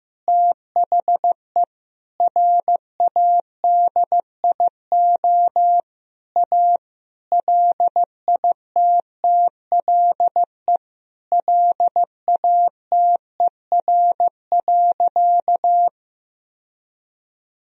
Morse code, heard as a keyed tone, about 14 wpm, 700 Hz. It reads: THE RADIO A LITTLE LATER.
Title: Sentences from Top 1000 Words - Review Part 3 - 15wpm